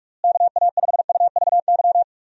Morse code, heard as a keyed tone, about 46 wpm, 700 Hz. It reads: KA5UVY
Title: USA Call Signs - 6 Characters - Repeated 2x - 45wpm